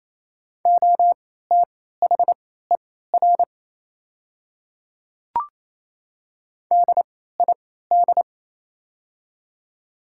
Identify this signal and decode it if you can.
Morse code, heard as a keyed tone, about 28 wpm, spaced as for 16 wpm, 700 Hz. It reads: OTHER DID
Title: 3x Character Spacing - Top 200 Words - 28wpm